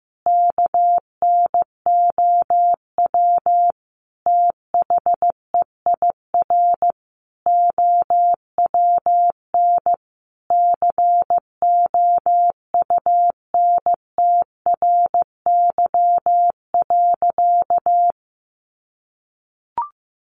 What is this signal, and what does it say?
Morse code, heard as a keyed tone, about 15 wpm, 700 Hz. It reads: KNOW THEIR OWN COUNTRY.
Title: Sentences from Top 300 Words 15wpm